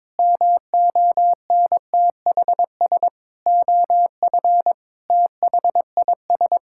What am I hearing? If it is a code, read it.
Morse code, heard as a keyed tone, about 22 wpm, 700 Hz. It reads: MONTHS OF THIS
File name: practice / 22wpm / Sets of 8 Words - 22wpm